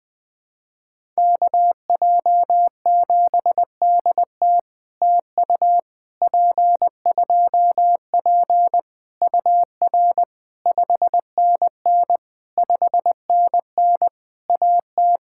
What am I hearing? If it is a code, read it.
Morse code, heard as a keyed tone, about 21 wpm, 700 Hz. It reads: KJ7DT TU P2P UR 5NN 5NN AT
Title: POTA QSO Practice - 20wpm